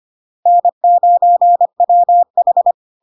Morse code, heard as a keyed tone, about 25 wpm, 700 Hz. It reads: N9WH
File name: Log-in-CW-25wpm-700-Hz